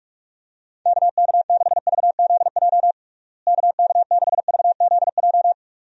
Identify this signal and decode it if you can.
Morse code, heard as a keyed tone, about 46 wpm, 700 Hz. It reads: KK6VZJ KK6VZJ
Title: USA Call Signs - 6 Characters - Repeated 2x - 45wpm